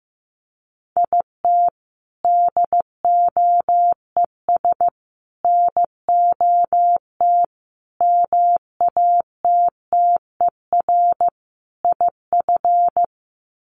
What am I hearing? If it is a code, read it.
Morse code, heard as a keyed tone, about 15 wpm, 700 Hz. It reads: IT DOES NOT MATTER IF
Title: Sets of 5 Words Encore 15wpm